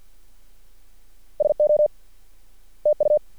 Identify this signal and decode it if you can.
Morse code, heard as a keyed tone, about 50 wpm, 600 Hz. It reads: SO TU
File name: archived / CRSnet